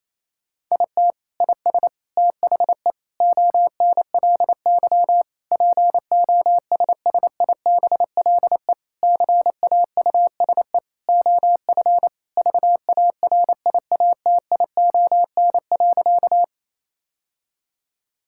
Morse code, heard as a keyed tone, about 28 wpm, 700 Hz. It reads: IT IS THE ONLY POSSIBLE CAUSE OF VARIATION.
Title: Sentences from Top 1600 Words 28wpm